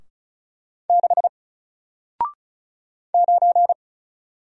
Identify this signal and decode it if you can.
Morse code, heard as a keyed tone, about 36 wpm, 700 Hz. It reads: 6 9
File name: Single-Letter-Number-35wpm